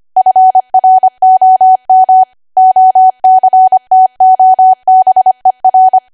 Morse code, heard as a keyed tone, about 25 wpm, 750 Hz. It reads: FROM OCTOBER
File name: Archive / 25 WPM / 25WPM